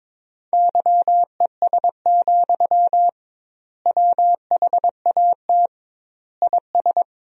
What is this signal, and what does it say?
Morse code, heard as a keyed tone, about 22 wpm, 700 Hz. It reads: YES, WHAT IS
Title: 2x Word-Spacing - Sentences from Top 500 Words - 22wpm